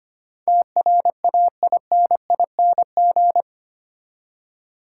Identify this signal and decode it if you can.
Morse code, heard as a keyed tone, about 25 wpm, 700 Hz. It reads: TRAINING
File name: Eight Letter Words - 25wpm